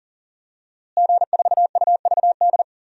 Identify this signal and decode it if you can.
Morse code, heard as a keyed tone, about 40 wpm, 700 Hz. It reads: G4UVD
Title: DXCC-100 Call Signs - 5 Characters - Repeated 2x - 40wpm